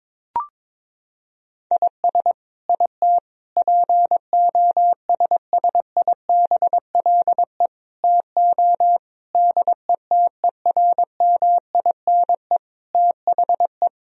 Morse code, heard as a keyed tone, about 22 wpm, 700 Hz. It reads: IS IT POSSIBLE TO DETERMINE THE
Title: Sentences from Top 1000 Words - Review Part 4 - 22wpm